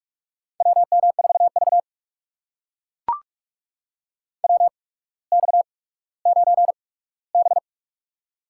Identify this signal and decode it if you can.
Morse code, heard as a keyed tone, about 44 wpm, spaced as for 19 wpm, 700 Hz. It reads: WM4V WX9B
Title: USA Call Signs - 4 Characters - 8x Character Spacing - 45wpm